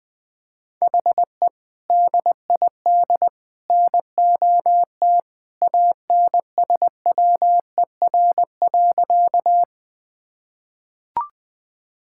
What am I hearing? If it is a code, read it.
Morse code, heard as a keyed tone, about 21 wpm, 700 Hz. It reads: HE DID NOT ANSWER.
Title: Sentences with 3-Word Phrases - 20wpm